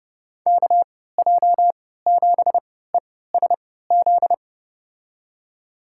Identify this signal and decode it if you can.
Morse code, heard as a keyed tone, about 30 wpm, spaced as for 17 wpm, 700 Hz. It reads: KJ7ESZ
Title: USA Call Signs - 3x Character Spacing - 30wpm